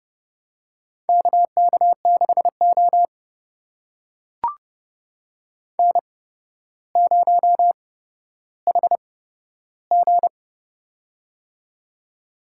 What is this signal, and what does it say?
Morse code, heard as a keyed tone, about 32 wpm, spaced as for 13 wpm, 700 Hz. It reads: KK6O N0HG
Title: USA Call Signs - 4 Characters - 8x Character Spacing - 30wpm